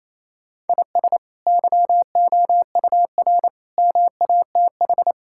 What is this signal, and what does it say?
Morse code, heard as a keyed tone, about 27 wpm, 700 Hz. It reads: IS YOUR MATH
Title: Sentences from Top 1000 Words - Review Part 6 - 28wpm